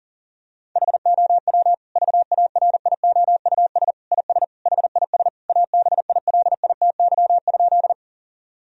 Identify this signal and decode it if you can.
Morse code, heard as a keyed tone, about 40 wpm, 700 Hz. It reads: HOW VARIOUS IS HIS ABILITY?
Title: Sentences from Top 1000 Words - Review Part 4 - 40wpm